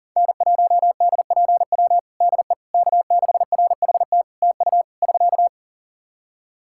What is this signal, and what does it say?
Morse code, heard as a keyed tone, about 40 wpm, 700 Hz. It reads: N1DPW DE K6RHT TU <SK>
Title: Practice QSO - Exchange 4 - Ending - 40wpm